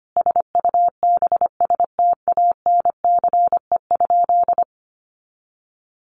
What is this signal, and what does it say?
Morse code, heard as a keyed tone, about 25 wpm, 700 Hz. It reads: SUBSTANCE?
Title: Sentences from Top 1500 Words 25wpm